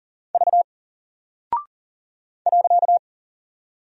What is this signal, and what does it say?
Morse code, heard as a keyed tone, about 40 wpm, 700 Hz. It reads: V .
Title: Single Character (Letter:Number:Punctuation:Prosign) 40wpm